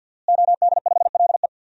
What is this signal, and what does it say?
Morse code, heard as a keyed tone, about 52 wpm, 700 Hz. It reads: OD5LE
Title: DXCC-100 Call Signs - 5 Characters - Repeated 2x - 50wpm